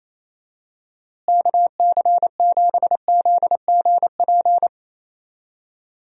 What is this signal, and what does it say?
Morse code, heard as a keyed tone, about 28 wpm, 700 Hz. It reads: KC7ZGP
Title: USA Call Signs - 6 Characters - Repeated 2x - 28wpm